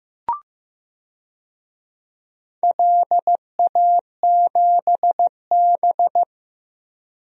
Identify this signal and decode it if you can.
Morse code, heard as a keyed tone, about 15 wpm, 700 Hz. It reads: LA7B
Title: DXCC-100 Call Signs - 4 Characters - Repeated 2x - 15wpm